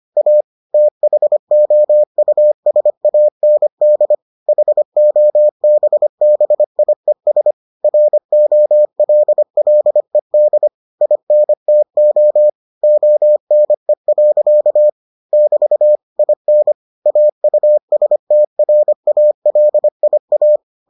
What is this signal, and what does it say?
Morse code, heard as a keyed tone, about 25 wpm, 600 Hz. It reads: A THOUSAND HOBBIES ROLLED INTO ONE. <BT> IN AUSTRALIA